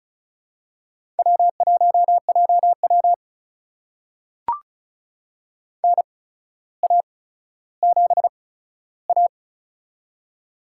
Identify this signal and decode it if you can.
Morse code, heard as a keyed tone, about 34 wpm, spaced as for 15 wpm, 700 Hz. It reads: W1JW NA7A
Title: USA Call Signs - 4 Characters - 8x Character Spacing - 35wpm